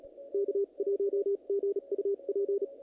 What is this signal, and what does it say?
Morse code, heard as a keyed tone, about 36 wpm, spaced as for 31 wpm, 400 Hz, with radio noise. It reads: K1GUP
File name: contests / naqp-cw-k5tr-op-k5pi- -2000